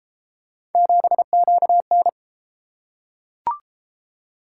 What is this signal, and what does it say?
Morse code, heard as a keyed tone, about 32 wpm, 700 Hz. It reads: 7QN